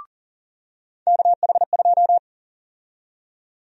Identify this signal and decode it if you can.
Morse code, heard as a keyed tone, about 40 wpm, 700 Hz. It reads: KH2
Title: USA Call Sign Prefixes - 40wpm